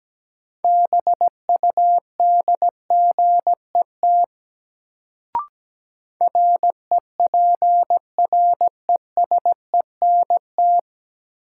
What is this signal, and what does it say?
Morse code, heard as a keyed tone, about 17 wpm, 700 Hz. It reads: BUDGET REPRESENT
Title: Sentences from Top 1200 Words - New Word Drill - 17wpm